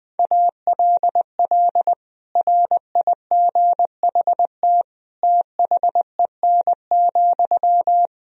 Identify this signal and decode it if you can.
Morse code, heard as a keyed tone, about 21 wpm, 700 Hz. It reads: ALL RIGHT THEN,
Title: Sentences from Top 500 Words and Abbreviations 20wpm